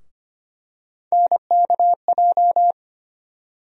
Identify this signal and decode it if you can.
Morse code, heard as a keyed tone, about 25 wpm, 700 Hz. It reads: NKJ
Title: USA Call Sign Postfixes - 3 Letters 25wpm